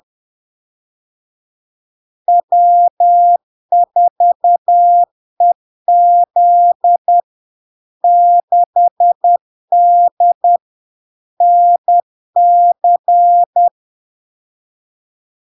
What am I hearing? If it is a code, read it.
Morse code, heard as a keyed tone, about 10 wpm, 700 Hz. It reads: W4EZ 6D NC
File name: LARK-FD-2022-Log-in-CW-10wpm-700-Hz